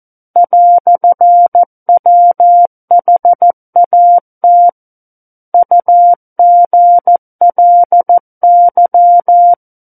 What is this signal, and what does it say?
Morse code, heard as a keyed tone, about 14 wpm, 700 Hz. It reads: "WHAT UGLY
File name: Book One / 05-Chapter 5 - 14 WPM